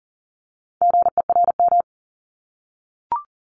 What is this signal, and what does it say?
Morse code, heard as a keyed tone, about 42 wpm, 700 Hz. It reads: GERM
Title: Multi-letter Phonemes 40wpm